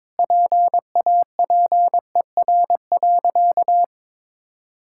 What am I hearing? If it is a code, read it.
Morse code, heard as a keyed tone, about 21 wpm, 700 Hz. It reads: PAPER.
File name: practice / 22wpm / Easy Sentences with Hints - 22wpm